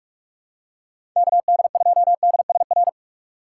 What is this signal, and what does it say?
Morse code, heard as a keyed tone, about 46 wpm, 700 Hz. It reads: KD2DSR
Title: USA Call Signs - 6 Characters - 3x Character Spacing - 45wpm